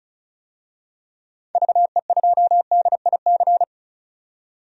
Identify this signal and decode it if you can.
Morse code, heard as a keyed tone, about 36 wpm, 700 Hz. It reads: VE2DIC
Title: DXCC-100 Call Signs - 6 Characters - 35wpm